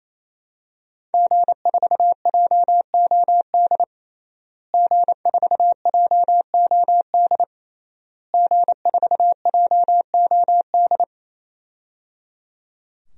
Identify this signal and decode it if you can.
Morse code, heard as a keyed tone, about 28 wpm, 700 Hz. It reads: G4JOD G4JOD G4JOD
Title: DXCC-100 Call Signs - 5 Characters - Repeated 3x - 28wpm